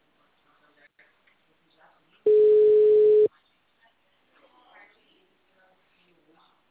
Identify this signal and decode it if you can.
Ringback tone (Europe, 425 Hz), with speech.
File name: out- -2049- - - .21017